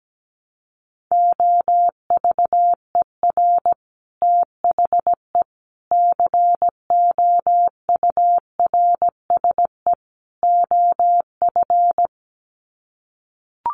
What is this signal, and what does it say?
Morse code, heard as a keyed tone, about 17 wpm, 700 Hz. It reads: OVER THE COURSE OF
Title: Sets of 4 Words 17wpm